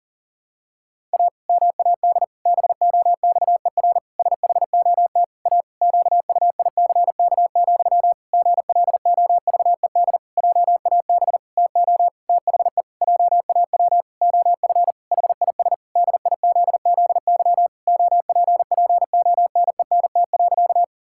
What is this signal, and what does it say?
Morse code, heard as a keyed tone, about 37 wpm, 700 Hz. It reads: A MAD BOXER SHOT A QUICK, GLOVED JAB TO THE JAW OF HIS DIZZY OPPONENT.
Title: Pangram Challenge v2 - 40wpm